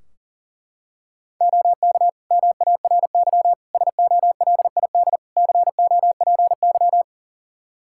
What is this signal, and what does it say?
Morse code, heard as a keyed tone, about 40 wpm, 700 Hz. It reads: OK MARY SOLID COPY
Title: Practice QSO - Exchange 3 - More Info - 40wpm